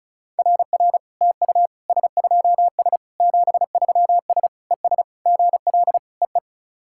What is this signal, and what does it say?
Morse code, heard as a keyed tone, about 36 wpm, 700 Hz. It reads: RR TU S2S 73S ES GL EE
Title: SOTA QSO Practice 35wpm